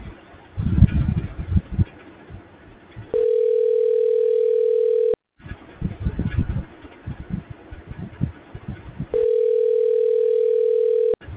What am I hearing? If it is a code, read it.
Ringback tone (North America), 440 + 480 Hz.